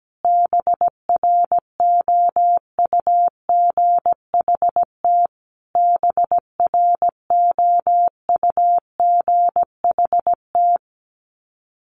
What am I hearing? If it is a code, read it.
Morse code, heard as a keyed tone, about 17 wpm, 700 Hz. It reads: BROUGHT BROUGHT
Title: Top 500 Words Repeated 17wpm